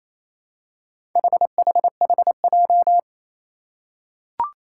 Morse code, heard as a keyed tone, about 27 wpm, 700 Hz. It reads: HHHJ
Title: Four Challenging Letter-Number Combinations 28wpm